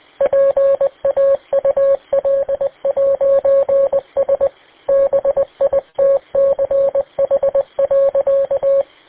Morse code, heard as a keyed tone, about 20 wpm, 550 Hz. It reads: PAUL'S BITCH.